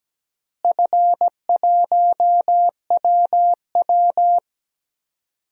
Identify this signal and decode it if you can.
Morse code, heard as a keyed tone, about 17 wpm, 700 Hz. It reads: F1WW
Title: DXCC-100 Call Signs - 4 Characters - Repeated 3x - 17wpm